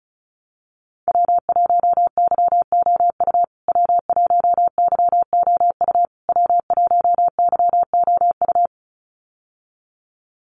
Morse code, heard as a keyed tone, about 34 wpm, 700 Hz. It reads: W1YOU W1YOU W1YOU
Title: USA Call Signs Repeated 3x 35wpm